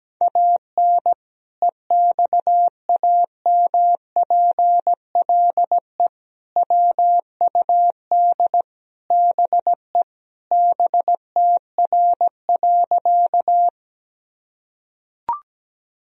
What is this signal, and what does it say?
Morse code, heard as a keyed tone, about 17 wpm, 700 Hz. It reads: AN EXAMPLE WUD BE BTR.
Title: Sentences from Top 400 Words and Abbreviations 17wpm